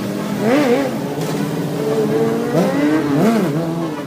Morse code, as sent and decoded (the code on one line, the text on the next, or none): none